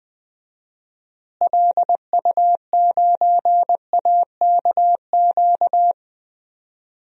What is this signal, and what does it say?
Morse code, heard as a keyed tone, about 20 wpm, 700 Hz. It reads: LU9AKQ
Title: DXCC-100 Call Signs - 6 Characters - Repeated 3x - 20wpm